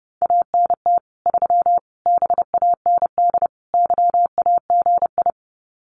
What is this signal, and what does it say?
Morse code, heard as a keyed tone, about 28 wpm, 700 Hz. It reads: ANT 3 BAND YAGI
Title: Common QSO Phrases 30wpm